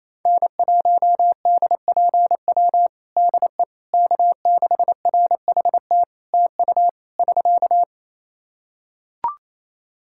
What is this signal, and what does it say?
Morse code, heard as a keyed tone, about 28 wpm, 700 Hz. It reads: N1DPW DE K6RHT TU <SK>
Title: Practice QSO - Exchange 4 - Ending - 28wpm